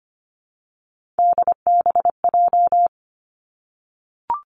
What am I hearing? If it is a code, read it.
Morse code, heard as a keyed tone, about 25 wpm, 700 Hz. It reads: DBJ
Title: Three Challenging Letter-Number Combinations 25wpm